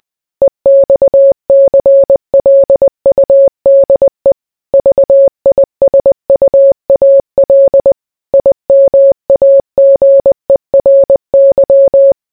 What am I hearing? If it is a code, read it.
Morse code, heard as a keyed tone, about 20 wpm, 550 Hz. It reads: EXCLUDE VISUAL IMAGERY